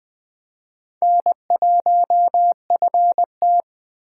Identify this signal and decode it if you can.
Morse code, heard as a keyed tone, about 20 wpm, 700 Hz. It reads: N1FT